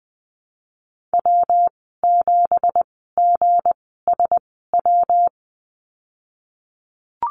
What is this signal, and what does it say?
Morse code, heard as a keyed tone, about 19 wpm, spaced as for 14 wpm, 700 Hz. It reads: W7GSW
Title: 193 Lesson 31g - Letter v - Call Signs 20wpm